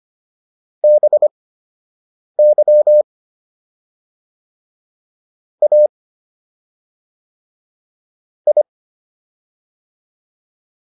Morse code, heard as a keyed tone, about 25 wpm, spaced as for 7 wpm, 600 Hz. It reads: BY A I